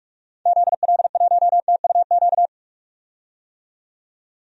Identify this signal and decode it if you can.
Morse code, heard as a keyed tone, about 47 wpm, 700 Hz. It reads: ZL1TUQ
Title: DXCC-100 Call Signs - 6 Characters - 45wpm